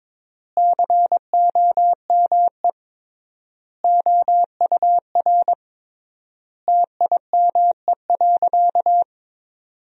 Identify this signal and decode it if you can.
Morse code, heard as a keyed tone, about 22 wpm, 700 Hz. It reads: COME OUR TIME.
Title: 3x Word-Spacing - Sentences from Top 200 Words - 22wpm